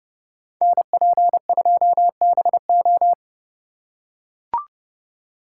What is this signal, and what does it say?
Morse code, heard as a keyed tone, about 31 wpm, 700 Hz. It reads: NP2BO